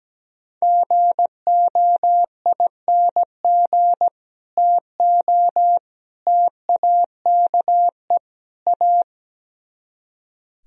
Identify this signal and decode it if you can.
Morse code, heard as a keyed tone, about 17 wpm, 700 Hz. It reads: GOING TO TAKE A
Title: Speed-Racing - Sets of 4 Words 17wpm